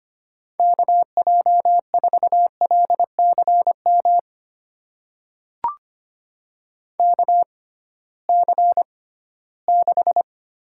Morse code, heard as a keyed tone, about 25 wpm, spaced as for 16 wpm, 700 Hz. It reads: KJ4LCM KC6
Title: USA Call Signs - 6x Character Spacing - 25wpm